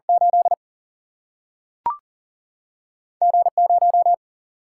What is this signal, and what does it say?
Morse code, heard as a keyed tone, about 40 wpm, 700 Hz. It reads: 8 G 0
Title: Single-Letter Single-Number 40wpm